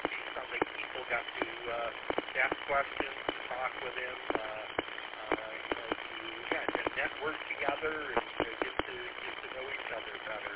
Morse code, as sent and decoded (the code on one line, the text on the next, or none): none